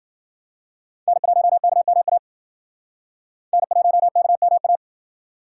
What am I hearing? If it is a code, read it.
Morse code, heard as a keyed tone, about 49 wpm, 700 Hz. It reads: N1KMA N1KMA
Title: Speed Racing - USA Call Signs 40wpm